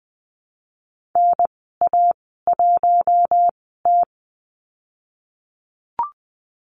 Morse code, heard as a keyed tone, about 20 wpm, spaced as for 14 wpm, 700 Hz. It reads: NA1T
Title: 154 Lesson 25g - Letter p - Call Signs 20wpm